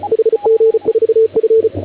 Morse code, heard as a keyed tone, about 33 wpm, 450 Hz, with radio noise. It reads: HG4F